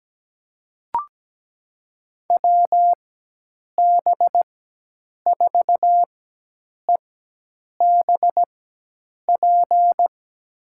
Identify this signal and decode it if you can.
Morse code, heard as a keyed tone, about 17 wpm, spaced as for 8 wpm, 700 Hz. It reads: WB4EBP
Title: USA Call Signs - 6 Characters - 4x Character Spacing - 17wpm